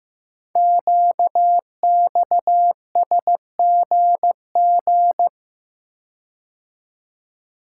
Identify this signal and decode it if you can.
Morse code, heard as a keyed tone, about 15 wpm, 700 Hz. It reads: QXSGG
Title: Code Groups - 5 Letter:Numbers - 15wpm